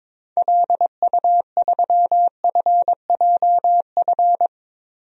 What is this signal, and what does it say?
Morse code, heard as a keyed tone, about 21 wpm, 700 Hz. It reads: LU3FJF
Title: DXCC-100 Call Signs - 6 Characters - Repeated 2x - 22wpm